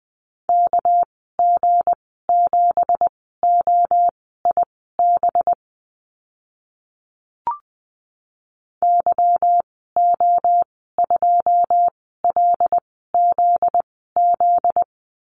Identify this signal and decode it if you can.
Morse code, heard as a keyed tone, about 20 wpm, spaced as for 15 wpm, 700 Hz. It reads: KG7OIB YO2LZZ
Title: 245 Lesson 39g - Letter z - Call Signs 20wpm